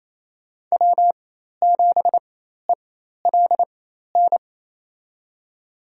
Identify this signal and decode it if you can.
Morse code, heard as a keyed tone, about 28 wpm, spaced as for 13 wpm, 700 Hz. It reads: W7ELN
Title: USA Call Signs - 5 Characters - 4x Character Spacing - 28wpm